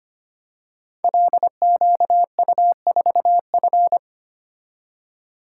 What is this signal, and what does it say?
Morse code, heard as a keyed tone, about 25 wpm, 700 Hz. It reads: LQU4F
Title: Code Groups - 5 Letter:Numbers - 25wpm